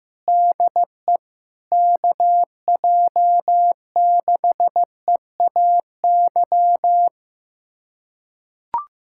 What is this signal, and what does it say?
Morse code, heard as a keyed tone, about 15 wpm, 700 Hz. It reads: DE KJ6EAY
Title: USA Callsign DE Callsign - 6 Characters - 15wpm